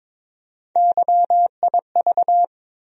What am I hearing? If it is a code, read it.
Morse code, heard as a keyed tone, about 22 wpm, 700 Hz. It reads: YIV
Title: USA Call Sign Postfixes - 3 Letters 22wpm